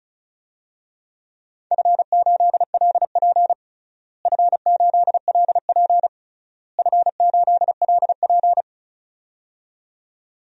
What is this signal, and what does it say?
Morse code, heard as a keyed tone, about 34 wpm, 700 Hz. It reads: F8LP F8LP F8LP
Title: DXCC-100 Call Signs - 4 Characters - Repeated 3x - 35wpm